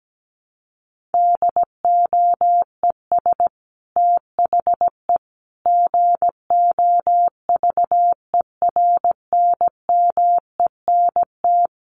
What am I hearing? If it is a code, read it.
Morse code, heard as a keyed tone, about 17 wpm, 700 Hz. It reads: DOES THE GOVERNMENT
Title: Sentences from Top 1500 Words 17wpm